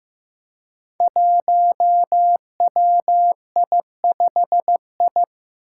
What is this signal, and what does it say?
Morse code, heard as a keyed tone, about 15 wpm, 700 Hz. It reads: 1WI5I
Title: Five Challenging Letter-Number Combinations 15wpm